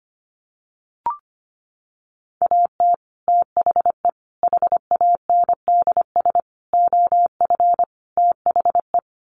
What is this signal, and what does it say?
Morse code, heard as a keyed tone, about 25 wpm, 700 Hz. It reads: AT THE HANDS OF THE
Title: Sets of 5 Words Encore2 25wpm